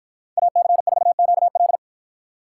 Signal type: Morse code

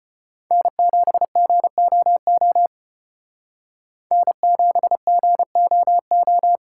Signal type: Morse code

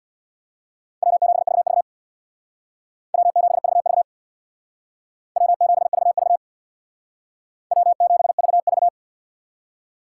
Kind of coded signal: Morse code

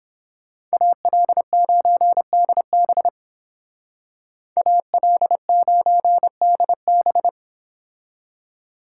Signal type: Morse code

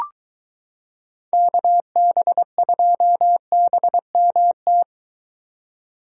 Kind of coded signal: Morse code

